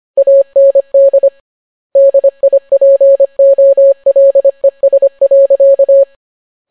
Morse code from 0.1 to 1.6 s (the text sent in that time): AND